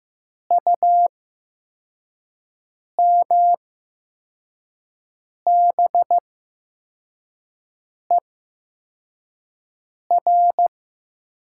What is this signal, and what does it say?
Morse code, heard as a keyed tone, about 15 wpm, spaced as for 4 wpm, 700 Hz. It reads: UMBER
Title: 8x Character Spacing - Top 100 Words - 15wpm